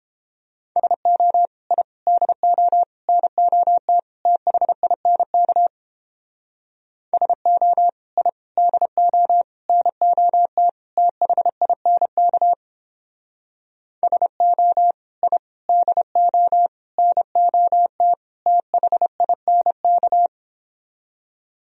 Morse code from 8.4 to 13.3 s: DO NOT THINK

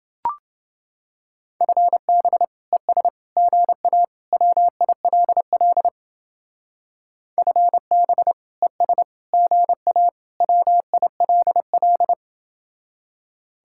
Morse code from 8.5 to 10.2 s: ES GA

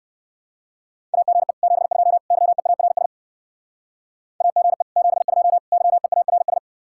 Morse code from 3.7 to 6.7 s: AGE 72 YEARS

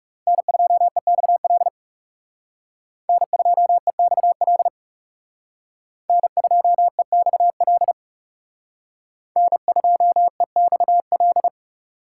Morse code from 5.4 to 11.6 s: N2EXL N2EXL

